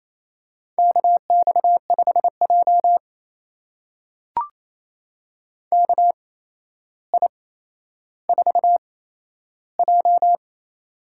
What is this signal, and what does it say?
Morse code, heard as a keyed tone, about 28 wpm, spaced as for 12 wpm, 700 Hz. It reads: KX5J KI4J
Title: USA Call Signs - 4 Characters - 8x Character Spacing - 28wpm